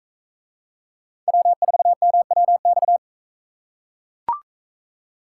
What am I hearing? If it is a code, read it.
Morse code, heard as a keyed tone, about 41 wpm, 700 Hz. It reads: W4MWX